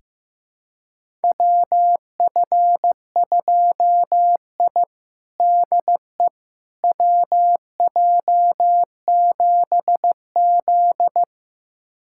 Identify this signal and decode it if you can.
Morse code, heard as a keyed tone, about 15 wpm, 700 Hz. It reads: WF2I DE WJ7Z
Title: USA Callsign DE Callsign - 4 Characters - 15wpm